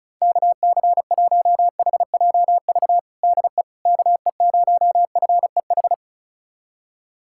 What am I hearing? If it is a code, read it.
Morse code, heard as a keyed tone, about 36 wpm, 700 Hz. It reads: KC1HJV DE KE0FEH
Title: USA Callsign DE Callsign 35wpm